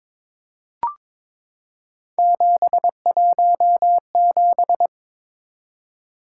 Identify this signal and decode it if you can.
Morse code, heard as a keyed tone, about 22 wpm, 700 Hz. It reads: 717
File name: practice / 22wpm / Three Letter-Number Combinations - 22wpm